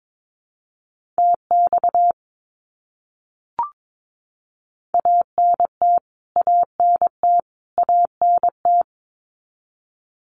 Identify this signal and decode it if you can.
Morse code, heard as a keyed tone, about 22 wpm, 700 Hz. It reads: TX ANT ANT ANT
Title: Abbreviations Repeated 3x - 22wpm